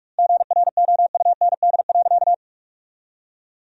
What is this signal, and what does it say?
Morse code, heard as a keyed tone, about 46 wpm, 700 Hz. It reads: GROUND.
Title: 2x Word-Spacing - Sentences from Top 500 Words - 45wpm